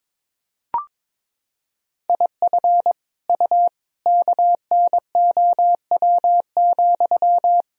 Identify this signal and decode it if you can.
Morse code, heard as a keyed tone, about 21 wpm, 700 Hz. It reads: IF U KNOW,
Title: Sentences from Top 300 Words and Abbreviations 22wpm